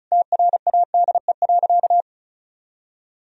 Morse code, heard as a keyed tone, about 36 wpm, 700 Hz. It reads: TRADE.
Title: Sentences from Top 1000 Words - Review Part 4 - 35wpm